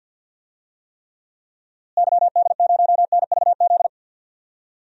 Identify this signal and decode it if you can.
Morse code, heard as a keyed tone, about 51 wpm, 700 Hz. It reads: YD0NVZ
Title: DXCC-100 Call Signs - 6 Characters - Repeated 3x - 50wpm